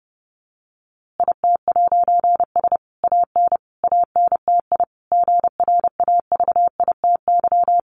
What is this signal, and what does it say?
Morse code, heard as a keyed tone, about 30 wpm, 700 Hz. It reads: IT'S AN ANTI GRAVITY
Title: A Sound Of Thunder 30wpm